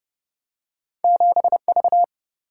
Morse code, heard as a keyed tone, about 30 wpm, 700 Hz. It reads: 7V
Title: Single-Letter Single-Number 30wpm